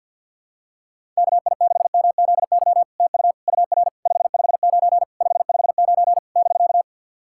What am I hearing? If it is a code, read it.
Morse code, heard as a keyed tone, about 50 wpm, 700 Hz. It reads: KI6MZY TU UR 559 559 <BK>
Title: SOTA QSO Practice 50wpm